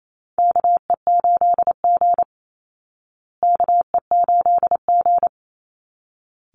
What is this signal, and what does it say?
Morse code, heard as a keyed tone, about 28 wpm, 700 Hz. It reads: KE8G KE8G
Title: Call Signs Repeated 3x - 28wpm